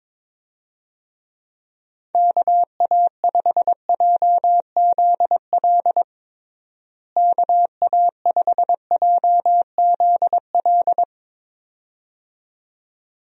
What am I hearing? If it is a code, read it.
Morse code, heard as a keyed tone, about 22 wpm, 700 Hz. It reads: KA5JZL KA5JZL
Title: USA Call Signs - 6 Characters - Repeated 2x - 22wpm